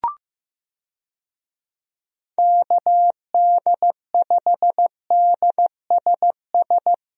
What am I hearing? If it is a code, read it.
Morse code, heard as a keyed tone, about 15 wpm, 700 Hz. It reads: KD5DSS
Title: USA Call Signs - 6 Characters - Repeated 2x - 15wpm